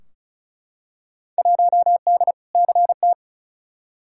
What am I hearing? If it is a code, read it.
Morse code, heard as a keyed tone, about 34 wpm, 700 Hz. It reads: 1D CT
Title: Field Day Exchanges - 35wpm